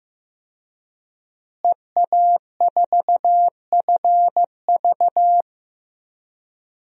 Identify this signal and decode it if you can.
Morse code, heard as a keyed tone, about 15 wpm, 700 Hz. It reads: EA4FV
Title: DXCC-100 Call Signs - 5 Characters - Repeated 2x - 15wpm